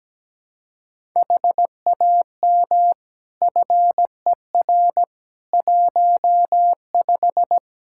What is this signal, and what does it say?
Morse code, heard as a keyed tone, about 17 wpm, 700 Hz. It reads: HAM FER 15
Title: Practice QSO - Exchange 3 - More Info - 17wpm